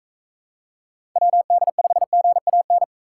Morse code, heard as a keyed tone, about 45 wpm, 700 Hz. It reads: WD5GAN